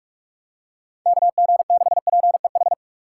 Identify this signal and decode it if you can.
Morse code, heard as a keyed tone, about 44 wpm, 700 Hz. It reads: KG6PEH